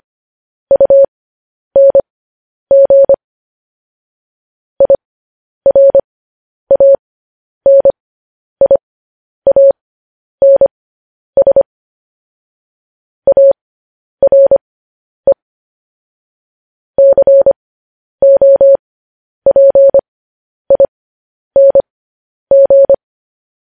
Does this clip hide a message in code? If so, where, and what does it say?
Yes: Morse code from 0.7 to 22.9 s, heard as a keyed tone, about 25 wpm, spaced as for 10 wpm, 550 Hz. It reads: UNG IRANIANS ARE COPING